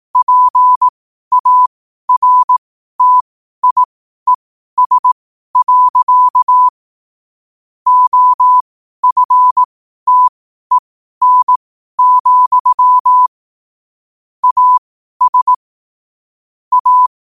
Morse code, heard as a keyed tone, about 18 wpm, spaced as for 12 wpm, 1000 Hz. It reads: PARTIES. OFTEN, AS A
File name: qotdfiles / 12wpm